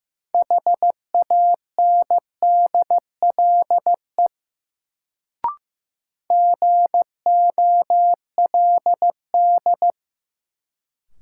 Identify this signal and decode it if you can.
Morse code, heard as a keyed tone, about 15 wpm, 700 Hz. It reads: HANDLE GOLD
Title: Sentences from Top 1100 Words - New Word Drill Alt - 15wpm